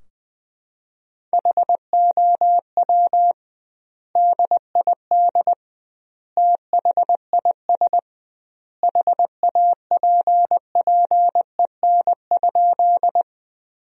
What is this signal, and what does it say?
Morse code, heard as a keyed tone, about 20 wpm, 700 Hz. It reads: HOW DID THIS HAPPEN?
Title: 2x Word-Spacing - Sentences from Top 400 Words - 20wpm